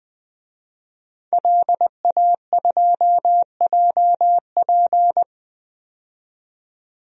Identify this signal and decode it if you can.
Morse code, heard as a keyed tone, about 20 wpm, 700 Hz. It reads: LA2JP
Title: DXCC-100 Call Signs - 5 Characters - 20wpm